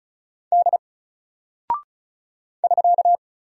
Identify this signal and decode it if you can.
Morse code, heard as a keyed tone, about 36 wpm, 700 Hz. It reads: D <SK>
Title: Single Character (Letter:Number:Punctuation:Prosign) Alt 35wpm